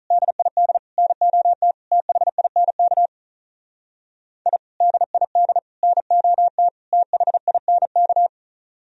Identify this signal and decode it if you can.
Morse code, heard as a keyed tone, about 43 wpm, 700 Hz. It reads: DID NOT THINK I DID NOT THINK
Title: Speed-Racing - Sets of 4 Words 35wpm